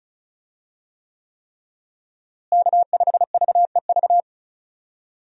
Morse code, heard as a keyed tone, about 34 wpm, 700 Hz. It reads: K5VEV
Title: USA Call Signs - 5 Characters - Repeated 3x - 35wpm